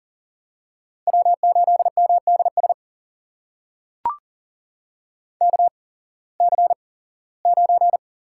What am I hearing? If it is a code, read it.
Morse code, heard as a keyed tone, about 40 wpm, spaced as for 21 wpm, 700 Hz. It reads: W8MDS KC9
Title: USA Call Signs - 8x Character Spacing - 40wpm